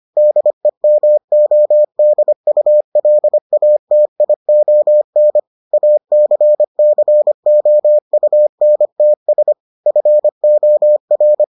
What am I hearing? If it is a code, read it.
Morse code, heard as a keyed tone, about 25 wpm, 600 Hz. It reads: DEMODULATION ACCOUNTS FOR